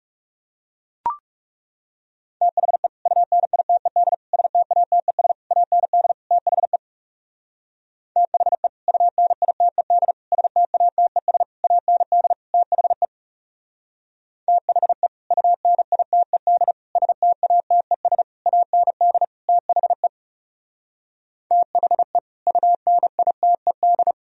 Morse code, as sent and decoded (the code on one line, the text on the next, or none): - .... / ..- -. .. - . -.. / ... - .- - . ... / .- -. -.. / - .... / - .... / ..- -. .. - . -.. / ... - .- - . ... / .- -. -.. / - .... / - .... / ..- -. .. - . -.. / ... - .- - . ... / .- -. -.. / - .... / - .... / ..- -. .. - . -..
THE UNITED STATES AND THE THE UNITED STATES AND THE THE UNITED STATES AND THE THE UNITED